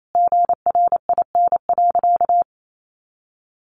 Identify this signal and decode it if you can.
Morse code, heard as a keyed tone, about 28 wpm, 700 Hz. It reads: GRIN.
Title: A Sound Of Thunder 28wpm